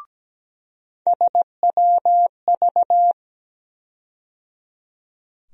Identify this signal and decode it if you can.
Morse code, heard as a keyed tone, about 17 wpm, 700 Hz. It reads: SWV
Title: USA Call Sign Postfixes - 3 Letters 17wpm